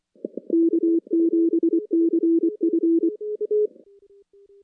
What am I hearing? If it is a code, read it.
Morse code, heard as a keyed tone, about 24 wpm, 330 Hz. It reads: K7CF
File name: contests / CW-SS-k5tr- -2300